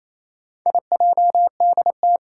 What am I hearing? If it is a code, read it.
Morse code, heard as a keyed tone, about 27 wpm, 700 Hz. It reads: IJDT
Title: Four Challenging Letter-Number Combinations 28wpm